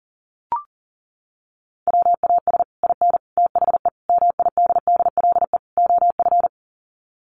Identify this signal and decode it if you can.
Morse code, heard as a keyed tone, about 40 wpm, 700 Hz. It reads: WAS IN THE MIDDLE OF
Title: Sets of 5 Words Encore 40wpm